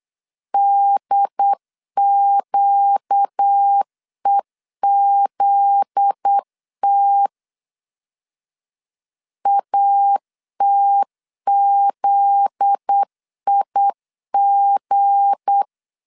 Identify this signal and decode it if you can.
Morse code, heard as a keyed tone, about 8 wpm, 800 Hz. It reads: DQEZT ATZIG
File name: LEZIONI / 7 GF9Z / 7 GF9Z 8wpm 3213